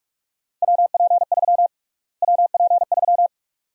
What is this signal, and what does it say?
Morse code, heard as a keyed tone, about 44 wpm, 700 Hz. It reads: WP3 WP3
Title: USA Call Sign Prefixes - Repeated 2x - 45wpm